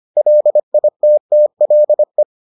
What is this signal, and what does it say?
Morse code, heard as a keyed tone, about 25 wpm, 600 Hz. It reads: LITTLE